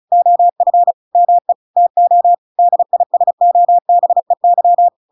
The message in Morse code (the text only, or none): OF ME TO DISOBEY